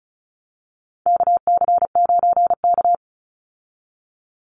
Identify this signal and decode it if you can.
Morse code, heard as a keyed tone, about 34 wpm, 700 Hz. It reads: KC9K
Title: Call Signs Repeated 2x - 35wpm